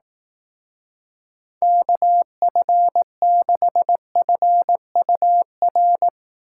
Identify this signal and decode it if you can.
Morse code, heard as a keyed tone, about 18 wpm, 700 Hz. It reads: KF6FUR
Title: Speed Racing - USA 2x3 Call Signs 15wpm